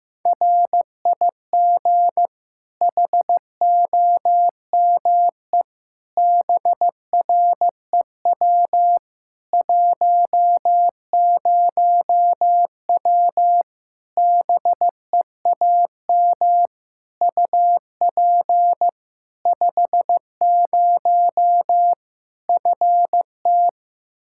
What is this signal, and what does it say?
Morse code, heard as a keyed tone, about 15 wpm, 700 Hz. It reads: RIG HOME BREW 10W BEAM UP 50 FT
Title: Practice QSO - Exchange 2 - Info - 15wpm